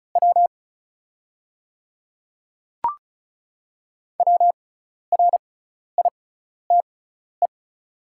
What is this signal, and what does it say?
Morse code, heard as a keyed tone, about 34 wpm, spaced as for 12 wpm, 700 Hz. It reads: W WRITE
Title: 6x Character Spacing - Top 300 Words - 35wpm